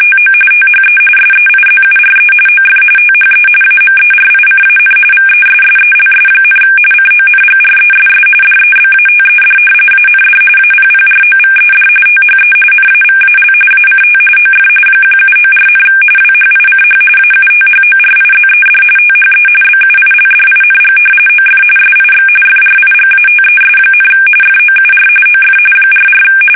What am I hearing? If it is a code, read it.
RTTY, about 45.45 baud. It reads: TO THE SEIZURE OF THE AMERICAN VESSELS, CONGRESS DEBATED AND APPROVED THE NAVAL ACT OF 1794, WHICH AUTHORIZED THE BUILDING OF SIX FRIGATES, FOUR OF 44
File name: fsk850 / usnavy1